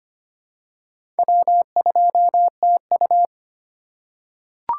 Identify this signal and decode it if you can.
Morse code, heard as a keyed tone, about 25 wpm, 700 Hz. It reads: W2TU
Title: USA Call Signs - 4x Character Spacing - 25wpm